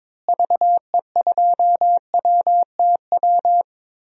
Morse code, heard as a keyed tone, about 21 wpm, 700 Hz. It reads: VE2WTW
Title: DXCC-100 Call Signs - 6 Characters - Repeated 2x - 22wpm